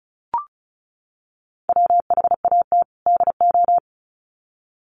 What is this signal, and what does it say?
Morse code, heard as a keyed tone, about 34 wpm, 700 Hz. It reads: WHAT DO
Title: Sets of 2 Words 35wpm